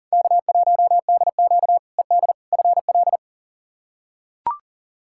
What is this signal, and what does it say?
Morse code, heard as a keyed tone, about 40 wpm, 700 Hz. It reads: K1DQ ED FL
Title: K1USN SST QSO Practice - 40wpm